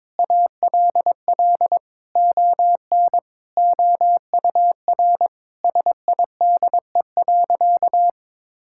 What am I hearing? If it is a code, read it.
Morse code, heard as a keyed tone, about 22 wpm, 700 Hz. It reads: ALL ON OUR SIDE.
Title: Sentences from Top 1000 Words - Review Part 6 - 22wpm